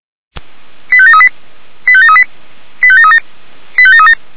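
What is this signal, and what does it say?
Five-tone selective call (ZVEI), heard as quick runs of five tones, tones of about 70 ms. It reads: 85738 85738 85737 85737
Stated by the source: ZVEI five-tone selective call 85723 received by SDR.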